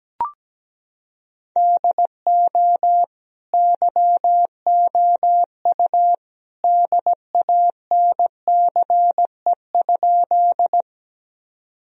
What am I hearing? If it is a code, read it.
Morse code, heard as a keyed tone, about 17 wpm, 700 Hz. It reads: DO YOU DANCE?
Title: Sentences from Top 1000 Words - Review Part 4 - 17wpm